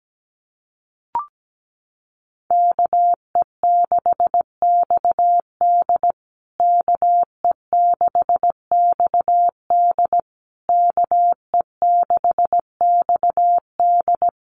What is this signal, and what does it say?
Morse code, heard as a keyed tone, about 17 wpm, 700 Hz. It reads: KE6XD KE6XD KE6XD
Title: USA Call Signs Repeated 3x 17wpm